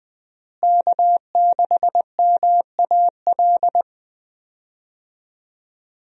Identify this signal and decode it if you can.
Morse code, heard as a keyed tone, about 20 wpm, 700 Hz. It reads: K6MAL
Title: USA Call Signs - 5 Characters - Repeated 3x - 20wpm